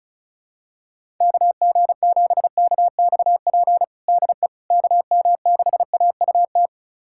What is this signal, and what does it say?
Morse code, heard as a keyed tone, about 34 wpm, 700 Hz. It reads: KG7KXP DE KM6AUT
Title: USA Callsign DE Callsign - 6 Characters - 35wpm